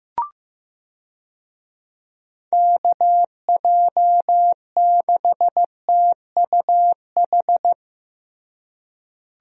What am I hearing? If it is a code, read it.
Morse code, heard as a keyed tone, about 15 wpm, 700 Hz. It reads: KJ6TUH
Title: USA Call Signs - 6 Characters - Repeated 3x - 15wpm